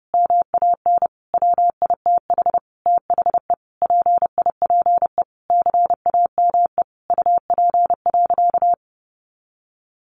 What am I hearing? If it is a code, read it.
Morse code, heard as a keyed tone, about 30 wpm, 700 Hz. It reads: MAN WITH THE PIPE CAME UP.
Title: Sentences from Top 1500 Words 30wpm